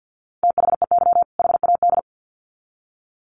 Morse code, heard as a keyed tone, about 50 wpm, 700 Hz. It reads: THEY HAD
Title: Sets of 2 Words 50wpm